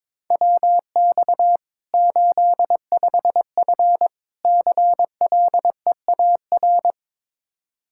Morse code, heard as a keyed tone, about 22 wpm, 700 Hz. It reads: WX 85F CLEAR
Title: Practice QSO - Exchange 2 - Info - 22wpm